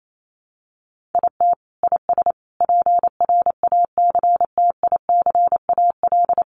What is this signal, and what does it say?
Morse code, heard as a keyed tone, about 28 wpm, 700 Hz. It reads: IT IS PRACTICAL
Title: Sentences from Top 1400 Words 28wpm